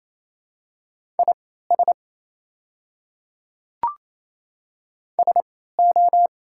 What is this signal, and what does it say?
Morse code, heard as a keyed tone, about 28 wpm, spaced as for 16 wpm, 700 Hz. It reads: IS SO
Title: 3x Character Spacing - Top 300 Words - 28wpm